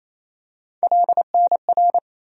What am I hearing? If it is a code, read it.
Morse code, heard as a keyed tone, about 28 wpm, 700 Hz. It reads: LNR
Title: USA Call Sign Postfixes - 3 Letters 28wpm